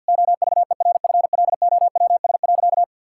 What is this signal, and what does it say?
Morse code, heard as a keyed tone, about 50 wpm, 700 Hz. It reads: OVERFLOWS.
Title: Pangram Challenge v2 - 50wpm